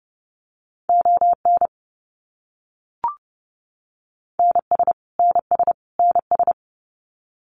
Morse code, heard as a keyed tone, about 28 wpm, 700 Hz. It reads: ON NS NS NS
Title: Canadian Provinces and Territories - Repeated 3x - 30wpm